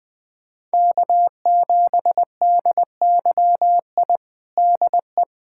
Morse code, heard as a keyed tone, about 20 wpm, 700 Hz. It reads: K7DYI DE
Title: USA Callsign DE Callsign - 5 Characters - 20wpm